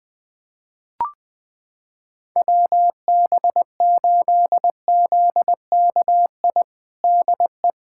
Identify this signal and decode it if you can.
Morse code, heard as a keyed tone, about 20 wpm, 700 Hz. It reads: WB8ZKI DE
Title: USA Callsign DE Callsign - 6 Characters - 20wpm